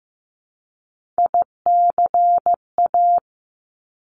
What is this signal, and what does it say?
Morse code, heard as a keyed tone, about 15 wpm, 700 Hz. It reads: ICA
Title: Top Three Letter Combinations in Words 15wpm